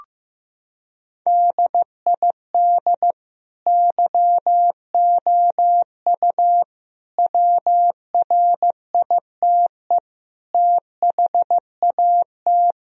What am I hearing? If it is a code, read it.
Morse code, heard as a keyed tone, about 15 wpm, 700 Hz. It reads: DID YOU WRITE THAT
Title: Sentences from Top 1000 Words - Review Part 3 - 15wpm